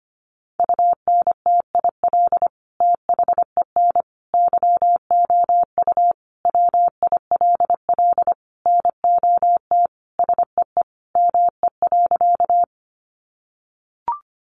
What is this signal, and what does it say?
Morse code, heard as a keyed tone, about 25 wpm, 700 Hz. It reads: UNTIL THEN YOU WILL NOT SEE ME.
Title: Sentences from Top 600 Words 25wpm